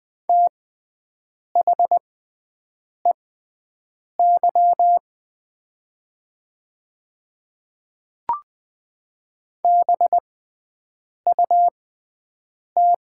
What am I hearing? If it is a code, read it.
Morse code, heard as a keyed tone, about 20 wpm, spaced as for 7 wpm, 700 Hz. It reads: THEY BUT
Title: 6x Character Spacing - Top 100 Words - 20wpm